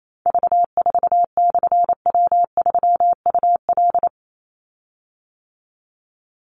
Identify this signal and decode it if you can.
Morse code, heard as a keyed tone, about 28 wpm, 700 Hz. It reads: V4/W3UL
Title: Call Signs - 28wpm